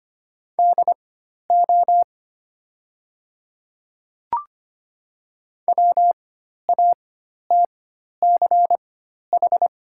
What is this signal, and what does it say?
Morse code, heard as a keyed tone, about 25 wpm, spaced as for 12 wpm, 690 Hz. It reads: DO WATCH
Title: 4x Character Spacing - Top 300 Words - 25wpm